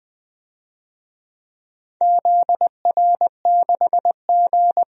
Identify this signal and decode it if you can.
Morse code, heard as a keyed tone, about 20 wpm, 700 Hz. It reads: ZR6G
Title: DXCC-100 Call Signs - 4 Characters - 20wpm